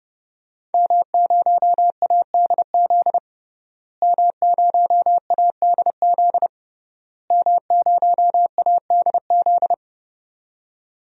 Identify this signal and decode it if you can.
Morse code, heard as a keyed tone, about 32 wpm, 700 Hz. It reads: M0ADZ M0ADZ M0ADZ
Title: DXCC-100 Call Signs - 5 Characters - Repeated 3x - 30wpm